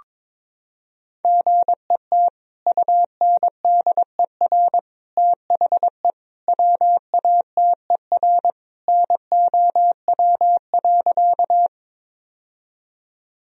Morse code, heard as a keyed tone, about 22 wpm, 700 Hz. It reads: GET UNDER THE WATER NOW.
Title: Sentences from Top 1000 Words - Review Part 1 - 22wpm